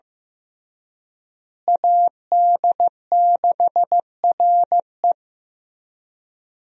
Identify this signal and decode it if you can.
Morse code, heard as a keyed tone, about 15 wpm, 700 Hz. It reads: AD6RE
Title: USA Call Signs - 5 Characters - Repeated 3x - 15wpm